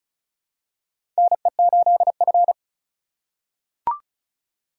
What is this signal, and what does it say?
Morse code, heard as a keyed tone, about 36 wpm, 690 Hz. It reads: NE8F
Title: USA Call Signs - 4 Characters - 2x Character Spacing - 35wpm